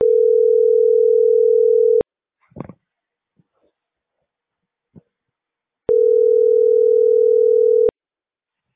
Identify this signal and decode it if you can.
Ringback tone (North America), 440 + 480 Hz.